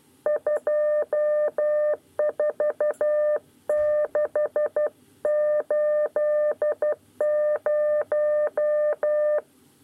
Morse code, heard as a keyed tone, about 12 wpm, 600 Hz. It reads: 24680
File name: 9 lezione cw 2